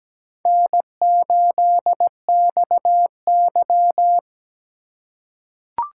Morse code, heard as a keyed tone, about 17 wpm, 700 Hz. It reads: N8XY
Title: USA Call Signs - 4 Characters - 2x Character Spacing - 17wpm